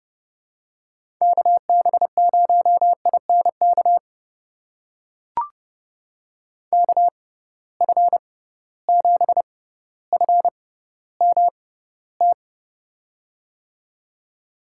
Morse code, heard as a keyed tone, about 30 wpm, spaced as for 15 wpm, 700 Hz. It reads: KB0INK KF7FMT
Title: USA Call Signs - 6 Characters - 6x Character Spacing - 30wpm